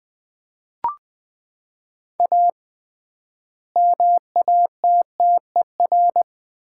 Morse code, heard as a keyed tone, about 20 wpm, 700 Hz. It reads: A MATTER
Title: 3x Word-Spacing - Sets of 3 Words - 20wpm